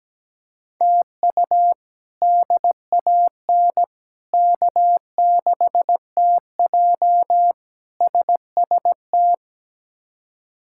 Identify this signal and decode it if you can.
Morse code, heard as a keyed tone, about 17 wpm, 700 Hz. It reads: TU DAN K6TJ SST
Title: K1USN SST QSO Practice - 17wpm